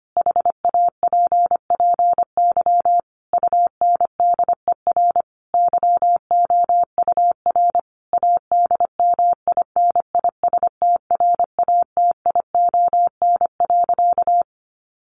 Morse code, heard as a keyed tone, about 25 wpm, 700 Hz. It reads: HAPPY UNDER YOUR ADMINISTRATION.
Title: Sentences from Top 1500 Words 25wpm